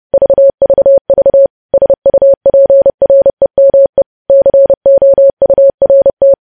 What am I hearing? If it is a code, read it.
Morse code, heard as a keyed tone, about 30 wpm, 550 Hz. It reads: VVV SUPREME COURT
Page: news headlines in morse code at 30 wpm